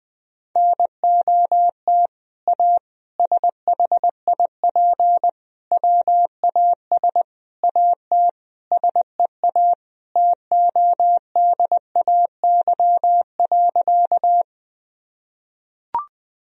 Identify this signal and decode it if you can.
Morse code, heard as a keyed tone, about 20 wpm, 700 Hz. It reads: NOT A SHIP WAS AT SEA TODAY.
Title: Sentences from Top 1000 Words - Review Part 2 - 20wpm